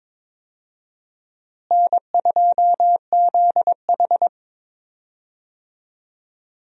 Morse code, heard as a keyed tone, about 22 wpm, 700 Hz. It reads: N2ZH
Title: USA Call Signs - 4 Characters - Repeated 3x - 22wpm